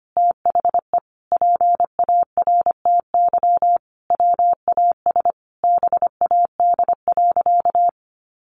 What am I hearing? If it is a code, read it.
Morse code, heard as a keyed tone, about 25 wpm, 700 Hz. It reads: THE PARTY WAS BAD.
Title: Sentences from Top 1500 Words 25wpm